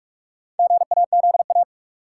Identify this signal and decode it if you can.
Morse code, heard as a keyed tone, about 46 wpm, 700 Hz. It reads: GAZA
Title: Sets of 5 Words - New Word Drill - 45wpm